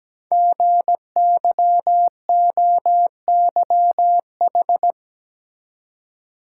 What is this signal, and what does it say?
Morse code, heard as a keyed tone, about 17 wpm, 700 Hz. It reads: GYOYH
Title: Code Groups - 5 Letter:Numbers - 17wpm